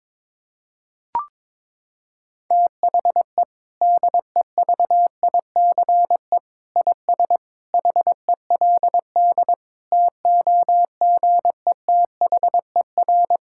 Morse code, heard as a keyed tone, about 22 wpm, 700 Hz. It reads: THE DEVICE IS HELD TOGETHER
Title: Sentences from Top 1600 Words 22wpm